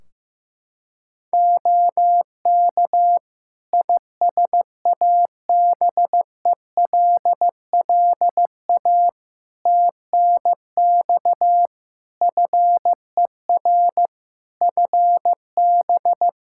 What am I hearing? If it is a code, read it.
Morse code, heard as a keyed tone, about 15 wpm, 700 Hz. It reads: OK ISABELLA TNX FER FB
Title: Practice QSO - Exchange 4 - Ending - 15wpm